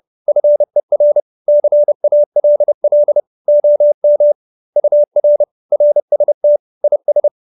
Morse code, heard as a keyed tone, about 30 wpm, 590 Hz. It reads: FER CALL OM UR RST IS
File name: QSO314 30